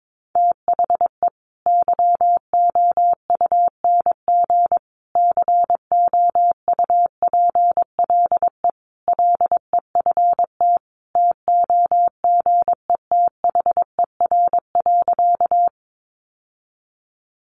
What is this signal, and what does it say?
Morse code, heard as a keyed tone, about 22 wpm, 700 Hz. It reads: THE YOUNG COUPLE LEFT TOGETHER.
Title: Sentences from Top 1100 Words 22wpm